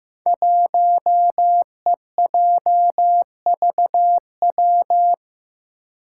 Morse code, heard as a keyed tone, about 15 wpm, 700 Hz. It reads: 1EJVW
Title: Five Challenging Letter-Number Combinations 15wpm